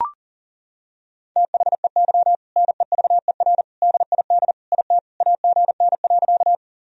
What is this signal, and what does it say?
Morse code, heard as a keyed tone, about 40 wpm, 700 Hz. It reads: THEY NEVER DID IT AGN.
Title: Sentences from Top 300 Words and Abbreviations 40wpm